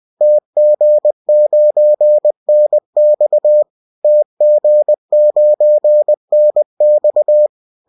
Morse code, heard as a keyed tone, about 20 wpm, 600 Hz. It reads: TG9NX TG9NX